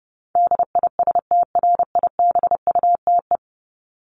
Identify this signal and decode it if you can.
Morse code, heard as a keyed tone, about 30 wpm, 700 Hz. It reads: DISTRIBUTE
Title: Top 1501-1750 Words 30wpm